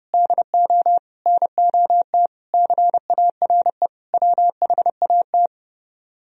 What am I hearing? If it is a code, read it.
Morse code, heard as a keyed tone, about 37 wpm, 700 Hz. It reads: DO NOT CARE WHAT
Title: Speed-Racing - Sets of 5 Words 30wpm